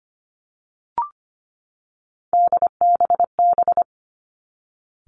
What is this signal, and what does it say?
Morse code, heard as a keyed tone, about 25 wpm, 700 Hz. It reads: DBB
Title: Three Challenging Letter-Number Combinations 25wpm